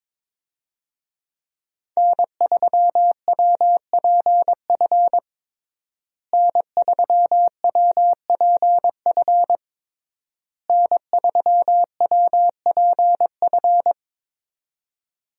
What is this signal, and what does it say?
Morse code, heard as a keyed tone, about 22 wpm, 700 Hz. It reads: N3WPF N3WPF N3WPF
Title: USA Call Signs - 5 Characters - Repeated 3x - 22wpm